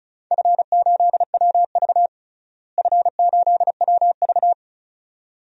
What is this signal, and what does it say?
Morse code, heard as a keyed tone, about 36 wpm, 700 Hz. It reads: F8WV F8WV
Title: DXCC-100 Call Signs - 4 Characters - Repeated 2x - 35wpm